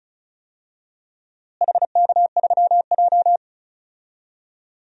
Morse code, heard as a keyed tone, about 36 wpm, 700 Hz. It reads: HK3J
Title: DXCC-100 Call Signs - 4 Characters - Repeated 3x - 35wpm